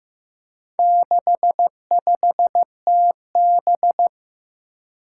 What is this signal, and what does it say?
Morse code, heard as a keyed tone, about 15 wpm, 700 Hz. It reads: 65TB
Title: Four Challenging Letter-Number Combinations 15wpm